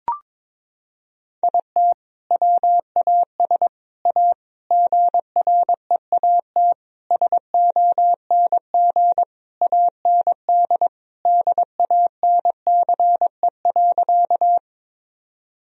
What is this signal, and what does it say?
Morse code, heard as a keyed tone, about 22 wpm, 700 Hz. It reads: IT WAS A GREAT SONG AND DANCE.
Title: Sentences from Top 1000 Words - Review Part 4 - 22wpm